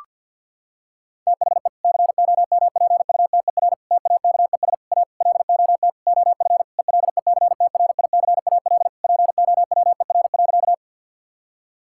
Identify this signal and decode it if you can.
Morse code, heard as a keyed tone, about 52 wpm, 700 Hz. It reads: THE COMPUTER TAKES A LOT OF ELECTRICAL POWER.
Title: Sentences from Top 1000 Words - Review Part 6 - 50wpm